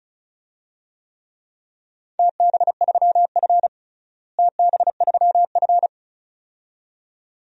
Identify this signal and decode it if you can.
Morse code, heard as a keyed tone, about 34 wpm, 700 Hz. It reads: TB3F TB3F
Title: DXCC-100 Call Signs - 4 Characters - Repeated 2x - 35wpm